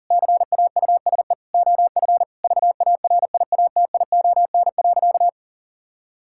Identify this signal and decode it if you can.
Morse code, heard as a keyed tone, about 40 wpm, 700 Hz. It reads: CAUSE OF VARIATION.
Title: Sentences from Top 1600 Words 40wpm